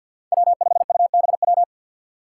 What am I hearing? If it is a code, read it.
Morse code, heard as a keyed tone, about 52 wpm, 700 Hz. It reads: W5UBW
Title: Five Challenging Letter-Number Combinations 50wpm